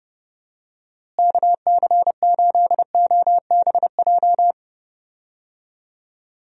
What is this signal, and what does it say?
Morse code, heard as a keyed tone, about 30 wpm, 700 Hz. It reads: KC8OBJ
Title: USA Call Signs - 6 Characters - Repeated 2x - 30wpm